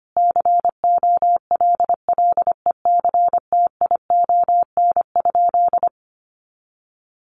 Morse code, heard as a keyed tone, about 25 wpm, 700 Hz. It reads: COLLECTION?
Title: Sentences from Top 1200 Words 25wpm